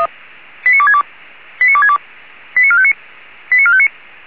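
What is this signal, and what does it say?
Five-tone selective call (ZVEI), heard as quick runs of five tones, tones of about 70 ms. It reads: 89262 89262 89459 89459
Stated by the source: ZVEI five-tone selective call 89269, received by SDR.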